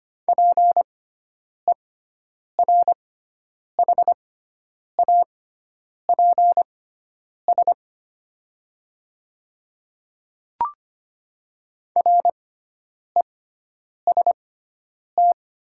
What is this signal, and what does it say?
Morse code, heard as a keyed tone, about 25 wpm, spaced as for 9 wpm, 700 Hz. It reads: PERHAPS REST
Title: 6x Character Spacing - Top 500 Words - 25wpm